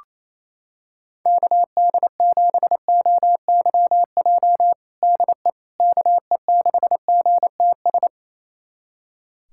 Morse code, heard as a keyed tone, about 28 wpm, 700 Hz. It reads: KD7OYJ DE KE6GTS
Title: USA Callsign DE Callsign - 6 Characters - 28wpm